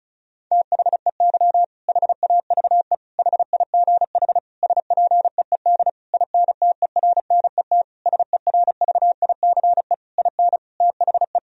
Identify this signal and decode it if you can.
Morse code, heard as a keyed tone, about 36 wpm, 700 Hz. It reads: THEY HAVE HIGH SPEED INTERNET SERVICE IN THE
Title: Sentences from Top 1000 Words - Review Part 4 - 35wpm